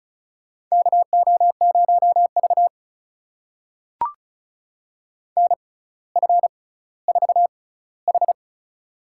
Morse code, heard as a keyed tone, about 35 wpm, spaced as for 18 wpm, 700 Hz. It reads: KO0V NF4H
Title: USA Call Signs - 4 Characters - 6x Character Spacing - 35wpm